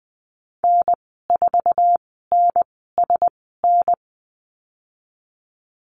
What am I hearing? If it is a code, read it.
Morse code, heard as a keyed tone, about 20 wpm, spaced as for 14 wpm, 700 Hz. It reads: N4NSN
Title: 048 Lesson 9g - Number 4 - Call Signs 20wpm